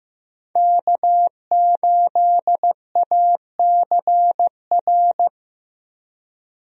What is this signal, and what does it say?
Morse code, heard as a keyed tone, about 16 wpm, 700 Hz. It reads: K8ACR
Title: USA Call Signs - 5 Characters - Repeated 2x - 15wpm